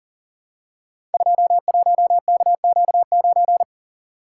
Morse code, heard as a keyed tone, about 42 wpm, 690 Hz. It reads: 21KQ9
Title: Code Groups - 5 Letter:Numbers - 40wpm